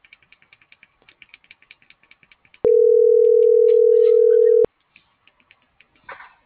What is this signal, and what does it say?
Ringback tone (North America), 440 + 480 Hz.